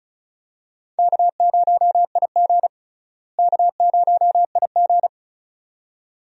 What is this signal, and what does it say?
Morse code, heard as a keyed tone, about 34 wpm, 700 Hz. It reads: K0IG K0IG
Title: USA Call Signs - 4 Characters - Repeated 2x - 35wpm